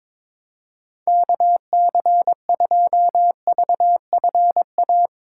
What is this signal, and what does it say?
Morse code, heard as a keyed tone, about 23 wpm, 700 Hz. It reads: KC2VFA
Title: USA Call Signs - 6 Characters - 2x Character Spacing - 22wpm